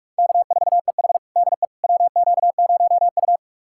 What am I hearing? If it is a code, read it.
Morse code, heard as a keyed tone, about 46 wpm, 700 Hz. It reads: K4EH DE WQ0U
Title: USA Callsign DE Callsign - 4 Characters - 45wpm